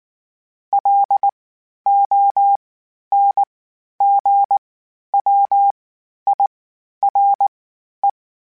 Morse code, heard as a keyed tone, about 19 wpm, spaced as for 11 wpm, 800 Hz. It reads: LONGWIRE
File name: (spaced) / FromSMs spaced mp3s / Antennas sp@18